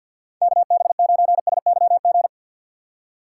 Morse code, heard as a keyed tone, about 52 wpm, 700 Hz. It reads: KB9SYG
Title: USA Call Signs - 6 Characters - 6x Character Spacing - 50wpm